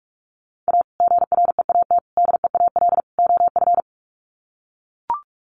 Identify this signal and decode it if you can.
Morse code, heard as a keyed tone, about 44 wpm, 700 Hz. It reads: A GREAT DEAL OF
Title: Sets of 4 Words 45wpm